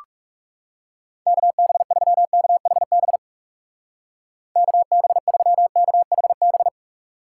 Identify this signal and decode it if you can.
Morse code, heard as a keyed tone, about 34 wpm, 700 Hz. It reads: KB3KHB KB3KHB
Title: Speed Racing - USA Call Signs 30wpm